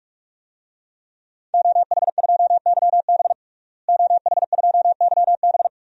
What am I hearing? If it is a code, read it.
Morse code, heard as a keyed tone, about 46 wpm, 700 Hz. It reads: OH2YB OH2YB
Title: DXCC-100 Call Signs - 5 Characters - Repeated 2x - 45wpm